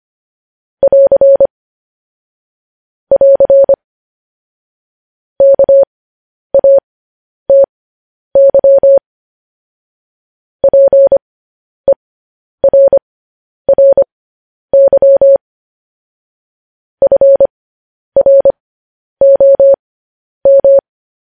Morse code, heard as a keyed tone, about 25 wpm, spaced as for 10 wpm, 550 Hz. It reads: <AR> <AR> KATY PERRY FROM
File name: podcast / News headlines-1x-25x10